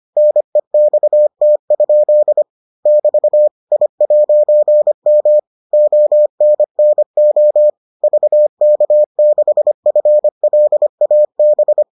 Morse code, heard as a keyed tone, about 25 wpm, 600 Hz. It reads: NEXT? <BT> I'M ONNO VK6FLAB